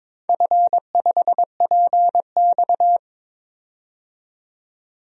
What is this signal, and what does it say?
Morse code, heard as a keyed tone, about 22 wpm, 700 Hz. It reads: F5PX
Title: DXCC-100 Call Signs - 4 Characters - 22wpm